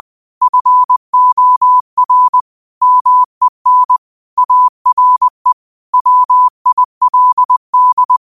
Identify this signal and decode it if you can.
Morse code, heard as a keyed tone, about 20 wpm, 1000 Hz. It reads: FOR MEN ARE WILD